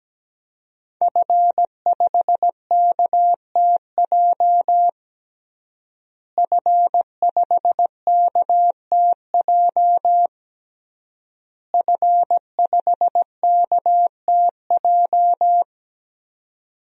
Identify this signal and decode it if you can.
Morse code, heard as a keyed tone, about 17 wpm, 700 Hz. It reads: F5KTJ F5KTJ F5KTJ
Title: DXCC-100 Call Signs - 5 Characters - Repeated 3x - 17wpm